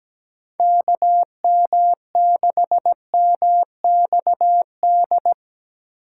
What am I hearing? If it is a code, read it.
Morse code, heard as a keyed tone, about 17 wpm, 700 Hz. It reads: KM6MXD
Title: USA Call Signs - 6 Characters - 2x Character Spacing - 17wpm